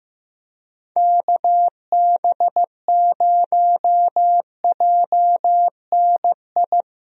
Morse code, heard as a keyed tone, about 15 wpm, 700 Hz. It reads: KB0JNI
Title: USA Call Signs - 6 Characters - Repeated 2x - 15wpm